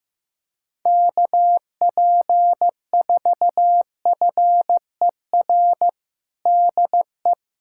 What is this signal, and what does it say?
Morse code, heard as a keyed tone, about 15 wpm, 700 Hz. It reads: KP4FER DE
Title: USA Callsign DE Callsign - 6 Characters - 15wpm